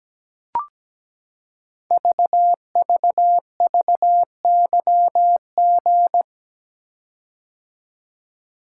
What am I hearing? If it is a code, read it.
Morse code, heard as a keyed tone, about 17 wpm, 700 Hz. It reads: VVVYG
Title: Code Groups - 5 Letters - 17wpm